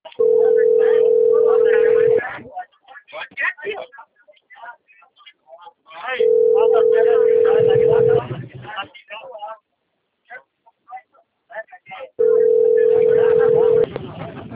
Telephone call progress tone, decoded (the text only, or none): ringback tone